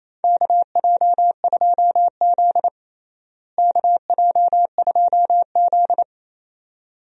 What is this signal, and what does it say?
Morse code, heard as a keyed tone, about 27 wpm, 700 Hz. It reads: KJ2Z KJ2Z
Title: USA Call Signs - 4 Characters - Repeated 2x - 28wpm